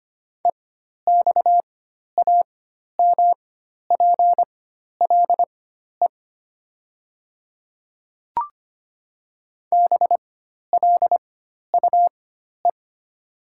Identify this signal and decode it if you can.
Morse code, heard as a keyed tone, about 25 wpm, spaced as for 12 wpm, 700 Hz. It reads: EXAMPLE BLUE
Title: 4x Character Spacing - Top 500 Words - 25wpm